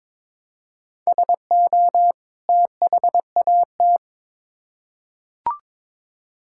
Morse code, heard as a keyed tone, about 22 wpm, 700 Hz. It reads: SO THAT
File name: Build A Phrase - 2-Word Phrases - 22wpm